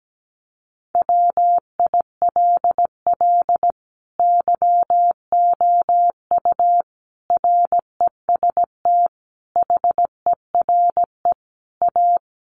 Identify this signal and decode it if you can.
Morse code, heard as a keyed tone, about 17 wpm, 700 Hz. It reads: WILL YOU REST HERE A
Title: Sentences from Top 500 Words 17wpm